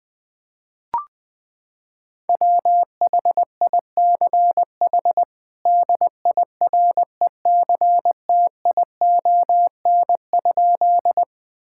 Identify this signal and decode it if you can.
Morse code, heard as a keyed tone, about 20 wpm, 700 Hz. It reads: WHICH DIRECTION?
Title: Sentences from Top 1000 Words - Review Part 5 - 20wpm